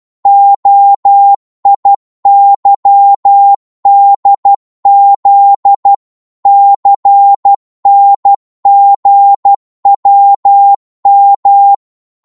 Morse code, heard as a keyed tone, about 12 wpm, 800 Hz. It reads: OIYDZ CNGWM